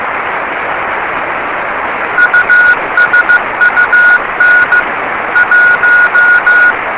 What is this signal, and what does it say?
Morse code, heard as a keyed tone, about 15 wpm, 1450 Hz. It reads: USUN 1
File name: yubileiry cw by dk3wn